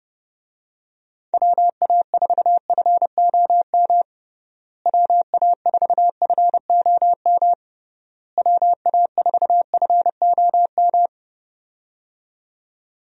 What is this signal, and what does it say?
Morse code, heard as a keyed tone, about 30 wpm, 700 Hz. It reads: WA4FOM WA4FOM WA4FOM
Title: USA Call Signs - 6 Characters - Repeated 3x - 30wpm